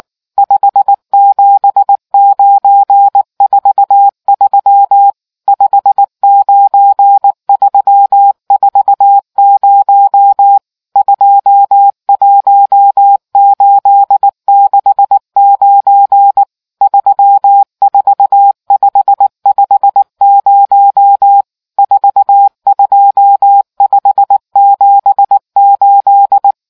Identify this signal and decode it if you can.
Morse code, heard as a keyed tone, about 19 wpm, 800 Hz. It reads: 57943 59340 21869 34550 42578